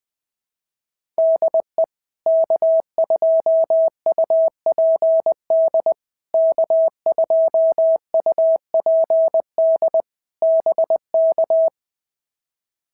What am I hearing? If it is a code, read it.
Morse code, heard as a keyed tone, about 20 wpm, 650 Hz. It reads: DE K2UPD K2UPD BK
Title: POTA QSO Practice - 20wpm